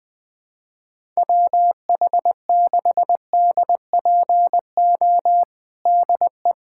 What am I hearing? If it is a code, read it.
Morse code, heard as a keyed tone, about 20 wpm, 700 Hz. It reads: WH6DPO DE
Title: USA Callsign DE Callsign 20wpm